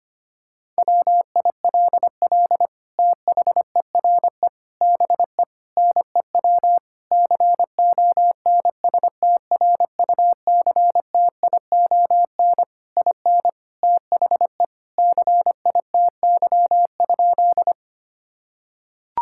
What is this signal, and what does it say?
Morse code, heard as a keyed tone, about 25 wpm, 700 Hz. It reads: WILL THERE BE NEW CONSTRUCTION IN THE CITY?
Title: Sentences from Top 1600 Words 25wpm